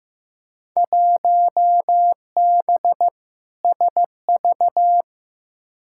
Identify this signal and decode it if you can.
Morse code, heard as a keyed tone, about 15 wpm, 700 Hz. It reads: 1B SV
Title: Field Day Exchanges Encore - 15wpm